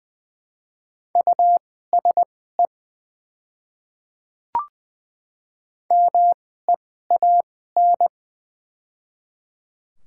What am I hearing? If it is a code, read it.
Morse code, heard as a keyed tone, about 20 wpm, spaced as for 14 wpm, 700 Hz. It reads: USE MEAN
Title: 2x Character Spacing - Top 200 Words - 20wpm